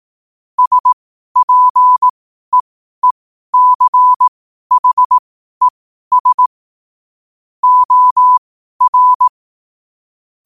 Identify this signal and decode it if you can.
Morse code, heard as a keyed tone, about 18 wpm, spaced as for 12 wpm, 1000 Hz. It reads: SPEECHES OR
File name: qotdfiles / 12wpm